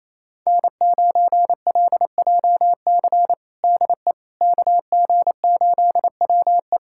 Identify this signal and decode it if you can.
Morse code, heard as a keyed tone, about 28 wpm, 700 Hz. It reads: N9LJC DE KG8WE
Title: USA Callsign DE Callsign - 5 Characters - 28wpm